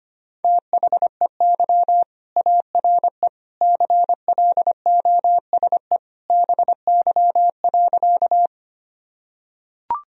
Morse code, heard as a keyed tone, about 25 wpm, 700 Hz. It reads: THEY ARE CLOSE BY.
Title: Sentences from Top 1000 Words - Review Part 1 - 25wpm